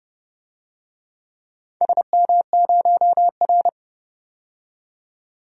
Morse code, heard as a keyed tone, about 30 wpm, 700 Hz. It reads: SM0R
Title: DXCC-100 Call Signs - 4 Characters - 30wpm